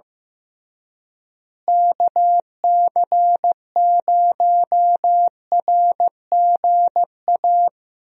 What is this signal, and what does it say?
Morse code, heard as a keyed tone, about 15 wpm, 700 Hz. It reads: KC0RGA
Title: USA Call Signs - 6 Characters - Repeated 2x - 15wpm